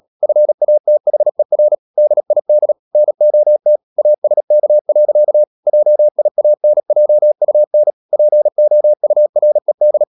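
Morse code, heard as a keyed tone, about 37 wpm, 600 Hz. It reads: FATHER DID NOT ASK. JIANJUN POURED
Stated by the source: Morse code practice at 35–39 wpm sending English text of Ken Follett's novel Never, from CW med Gnister.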